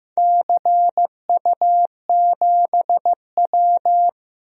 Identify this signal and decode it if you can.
Morse code, heard as a keyed tone, about 15 wpm, 700 Hz. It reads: CU7W
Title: DXCC-100 Call Signs - 4 Characters - 15wpm